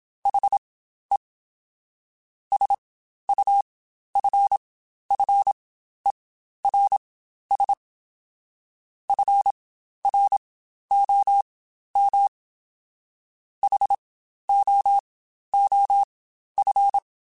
Morse code, heard as a keyed tone, about 27 wpm, spaced as for 12 wpm, 800 Hz. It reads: HE SUFFERS FROM HOOF
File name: Quotes Spaced / Quotes Spaced-mp3s / HoofInMouth sp @25